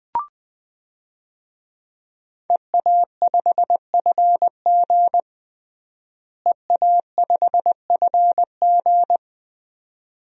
Morse code, heard as a keyed tone, about 21 wpm, 700 Hz. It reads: EA5FG EA5FG
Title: DXCC-100 Call Signs - 5 Characters - Repeated 2x - 20wpm